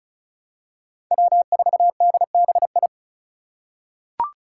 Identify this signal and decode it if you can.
Morse code, heard as a keyed tone, about 36 wpm, 700 Hz. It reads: W4DBI
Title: Five Challenging Letter-Number Combinations 35wpm